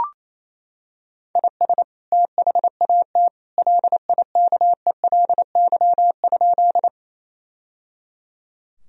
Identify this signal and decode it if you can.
Morse code, heard as a keyed tone, about 28 wpm, 700 Hz. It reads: IS THAT LIKELY?
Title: Sentences from Top 1000 Words - Review Part 4 - 28wpm